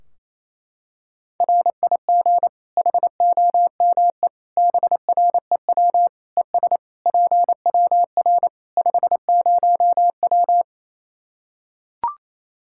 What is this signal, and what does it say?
Morse code, heard as a keyed tone, about 28 wpm, 700 Hz. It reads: RIG HOME BREW ES PWR 50W
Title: Practice QSO - Exchange 2 - Info - 28wpm